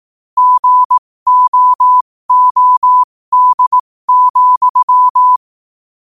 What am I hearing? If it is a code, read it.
Morse code, heard as a keyed tone, about 18 wpm, spaced as for 15 wpm, 1000 Hz. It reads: GOOD,
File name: qotdfiles / 15wpm